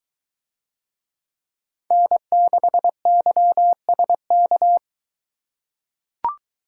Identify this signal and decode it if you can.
Morse code, heard as a keyed tone, about 22 wpm, 700 Hz. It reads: N6YSK